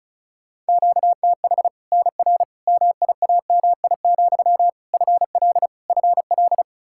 Morse code, heard as a keyed tone, about 36 wpm, 700 Hz. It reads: QTH NR MIAMI, FL FL
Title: Practice QSO - Exchange 1 - Beginning - 35wpm